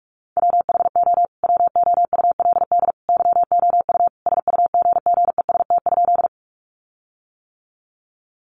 Morse code, heard as a keyed tone, about 44 wpm, 700 Hz. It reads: WHO WOULD YOU SUGGEST?
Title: Sentences from Top 1300 Words 45wpm